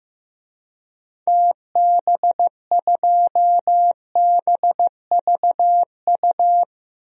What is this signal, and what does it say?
Morse code, heard as a keyed tone, about 15 wpm, 700 Hz. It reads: TB2BVU
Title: DXCC-100 Call Signs - 6 Characters - 15wpm